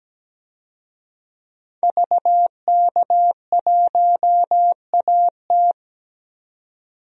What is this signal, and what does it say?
Morse code, heard as a keyed tone, about 17 wpm, 700 Hz. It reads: VK1AT
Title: DXCC-100 Call Signs - 5 Characters - 17wpm